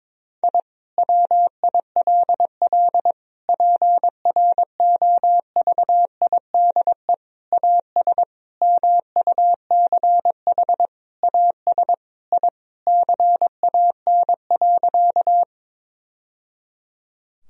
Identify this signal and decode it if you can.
Morse code, heard as a keyed tone, about 22 wpm, 700 Hz. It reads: I WILL PROVIDE AS MUCH AS I CAN.
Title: Sentences from Top 1000 Words - Review Part 3 - 22wpm